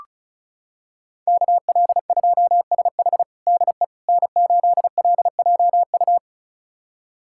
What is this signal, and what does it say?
Morse code, heard as a keyed tone, about 36 wpm, 700 Hz. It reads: KL2SH DE N8LJU
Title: USA Callsign DE Callsign 35wpm